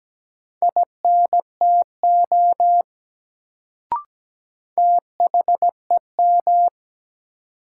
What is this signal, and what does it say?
Morse code, heard as a keyed tone, about 17 wpm, 700 Hz. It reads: INTO THEM
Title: Sets of 2 Words - New Word Drill - 17wpm